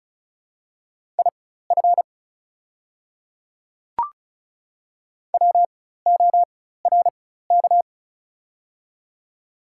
Morse code, heard as a keyed tone, about 34 wpm, spaced as for 16 wpm, 700 Hz. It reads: IF WORK
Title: 4x Character Spacing - Top 200 Words - 35wpm